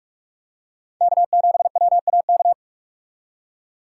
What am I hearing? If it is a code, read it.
Morse code, heard as a keyed tone, about 43 wpm, 700 Hz. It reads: K7WAK
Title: USA Call Signs - 5 Characters - 6x Character Spacing - 45wpm